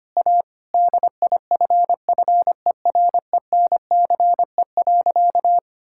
Morse code, heard as a keyed tone, about 25 wpm, 700 Hz. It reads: A DIFFERENCE.
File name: practice / 25wpm / Sentences with 3-Word Phrases - 25wpm